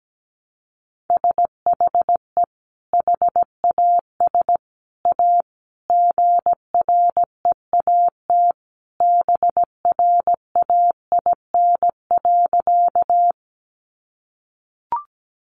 Morse code, heard as a keyed tone, about 17 wpm, 700 Hz. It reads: SHE HAS A GREAT BRAIN.
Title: Sentences from Top 1100 Words 17wpm